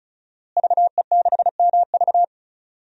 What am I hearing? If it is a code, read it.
Morse code, heard as a keyed tone, about 36 wpm, 700 Hz. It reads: VE6MV
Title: DXCC-100 Call Signs - 5 Characters - Repeated 3x - 35wpm